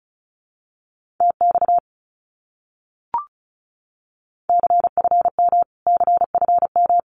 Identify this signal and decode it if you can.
Morse code, heard as a keyed tone, about 34 wpm, 700 Hz. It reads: TX CFM CFM
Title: Abbreviations Repeated 3x - 35wpm